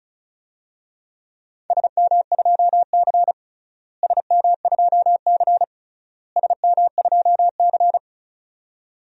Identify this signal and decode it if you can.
Morse code, heard as a keyed tone, about 34 wpm, 700 Hz. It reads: SM2C SM2C SM2C
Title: DXCC-100 Call Signs - 4 Characters - Repeated 3x - 35wpm